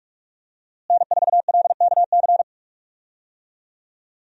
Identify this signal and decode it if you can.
Morse code, heard as a keyed tone, about 46 wpm, 700 Hz. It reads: N4LKC
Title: USA Call Signs - 5 Characters - Repeated 2x - 45wpm